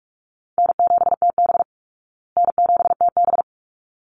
Morse code, heard as a keyed tone, about 44 wpm, 700 Hz. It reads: N7TB N7TB
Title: Easy Call Signs Repeated 2x - 45wpm